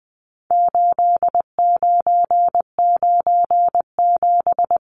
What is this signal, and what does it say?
Morse code, heard as a keyed tone, about 20 wpm, 700 Hz. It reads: 8997
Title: Numbers - 4-Digit 20wpm